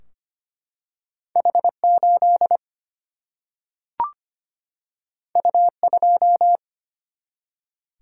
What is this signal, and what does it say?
Morse code, heard as a keyed tone, about 25 wpm, 700 Hz. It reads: H8 U2
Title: Single-Letter Single-Number 25wpm